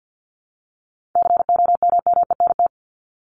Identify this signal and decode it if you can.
Morse code, heard as a keyed tone, about 53 wpm, 700 Hz. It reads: COMMENT
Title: Top 1751-2000 Words 50wpm